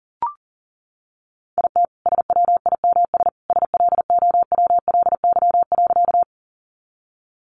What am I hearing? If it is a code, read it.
Morse code, heard as a keyed tone, about 40 wpm, 700 Hz. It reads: IT SWIMS SLOWLY.
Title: Easy Phrases 40wpm